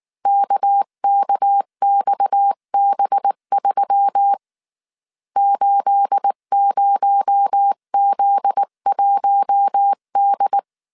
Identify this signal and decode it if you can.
Morse code, heard as a keyed tone, about 19 wpm, 800 Hz. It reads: KKXB3 80Z1D
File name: CW / DET / 1 9 1500C S B 18wpm 2551